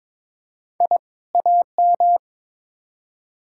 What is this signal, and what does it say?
Morse code, heard as a keyed tone, about 22 wpm, 700 Hz. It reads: I AM
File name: practice / 22wpm / Build A Phrase - 2-Word Phrases - 22wpm